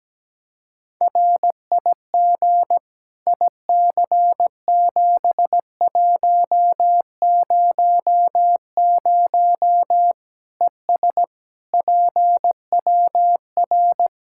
Morse code, heard as a keyed tone, about 17 wpm, 700 Hz. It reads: RIG IC7100 ES PWR
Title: Practice QSO - Exchange 2 - Info - 17wpm